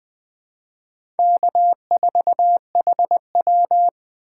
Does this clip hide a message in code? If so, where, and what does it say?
Yes: Morse code from 1.2 to 3.9 s, heard as a keyed tone, about 20 wpm, 700 Hz. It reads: K4HW